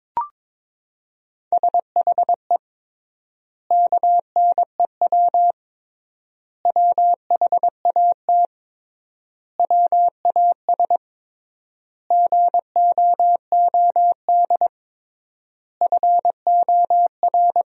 Morse code, heard as a keyed tone, about 22 wpm, 700 Hz. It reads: SHE KNEW WHAT WAS GOOD FOR
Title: 3x Word-Spacing - Sentences from Top 400 Words - 22wpm